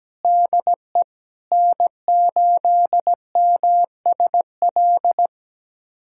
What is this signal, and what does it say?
Morse code, heard as a keyed tone, about 17 wpm, 700 Hz. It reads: DE N8MSL
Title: USA Callsign DE Callsign - 5 Characters - 17wpm